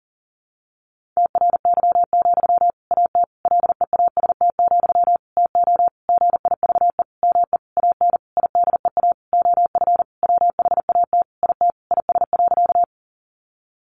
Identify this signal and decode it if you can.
Morse code, heard as a keyed tone, about 40 wpm, 700 Hz. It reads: TRY, AT LEAST, TO GIVE ME AN IDEA OF WHAT IT IS.
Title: Sentences from Top 700 Words 40wpm